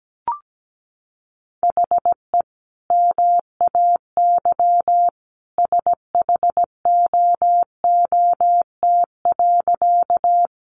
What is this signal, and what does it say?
Morse code, heard as a keyed tone, about 17 wpm, 700 Hz. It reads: HE MAY SHOOT.
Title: Sentences from Top 1200 Words 17wpm